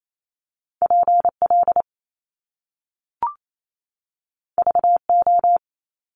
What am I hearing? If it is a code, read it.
Morse code, heard as a keyed tone, about 28 wpm, 700 Hz. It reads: PL VO
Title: Top Two Letter Combinations in Words 28wpm